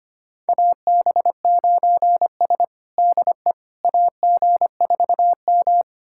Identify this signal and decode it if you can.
Morse code, heard as a keyed tone, about 25 wpm, 700 Hz. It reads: AB9S DE AG4M
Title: USA Callsign DE Callsign - 4 Characters - 25wpm